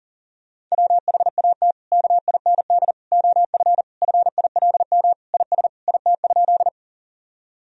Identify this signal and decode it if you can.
Morse code, heard as a keyed tone, about 40 wpm, 700 Hz. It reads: WHAT KIND OF FILM IS IT?
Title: Sentences from Top 1000 Words - Review Part 3 - 40wpm